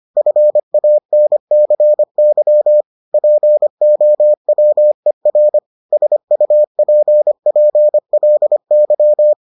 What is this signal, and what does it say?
Morse code, heard as a keyed tone, about 25 wpm, 600 Hz. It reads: FANCY POWER SUPPLY